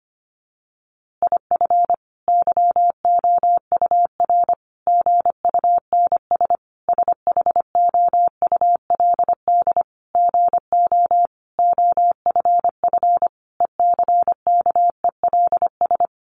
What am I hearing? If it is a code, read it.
Morse code, heard as a keyed tone, about 25 wpm, 700 Hz. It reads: IF YOUR GUNS SHOULD GO OFF ECKELS
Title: A Sound Of Thunder 25wpm